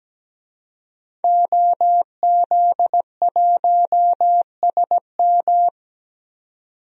Morse code, heard as a keyed tone, about 17 wpm, 700 Hz. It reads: OZ1SM
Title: DXCC-100 Call Signs - 5 Characters - 17wpm